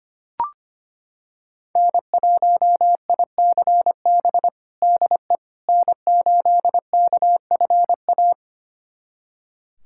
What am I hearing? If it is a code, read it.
Morse code, heard as a keyed tone, about 25 wpm, 700 Hz. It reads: N1ICB DE N8KFA
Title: USA Callsign DE Callsign - 5 Characters - 25wpm